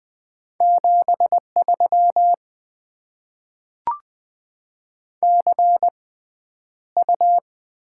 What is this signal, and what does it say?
Morse code, heard as a keyed tone, about 20 wpm, spaced as for 10 wpm, 690 Hz. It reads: 73 CU
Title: Top 20 QSO Elements - 6x Character Spacing - 20wpm